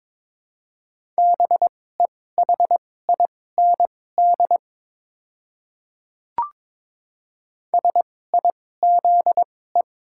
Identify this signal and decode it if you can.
Morse code, heard as a keyed tone, about 22 wpm, spaced as for 16 wpm, 700 Hz. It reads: BEHIND SIZE
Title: 2x Character Spacing - Top 500 Words - 22wpm